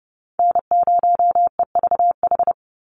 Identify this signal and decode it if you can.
Morse code, heard as a keyed tone, about 30 wpm, 700 Hz. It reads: N0EVH
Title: NA SOTA Chaser Callsigns 30wpm